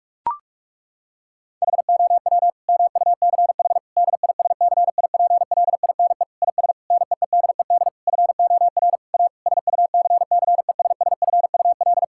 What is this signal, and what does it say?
Morse code, heard as a keyed tone, about 46 wpm, 700 Hz. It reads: HOW MUCH DISCIPLINE IS NEEDED FOR A SUCCESSFUL
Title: Sentences from Top 1000 Words - Review Part 6 - 45wpm